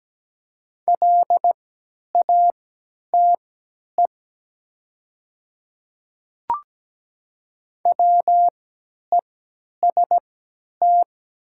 Morse code, heard as a keyed tone, about 17 wpm, spaced as for 10 wpm, 700 Hz. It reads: LATE WEST
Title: 3x Character Spacing - Top 500 Words - 17wpm